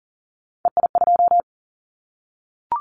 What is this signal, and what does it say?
Morse code, heard as a keyed tone, about 42 wpm, 700 Hz. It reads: EI2
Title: Callsign Prefixes for Top 50 DXCC - 40wpm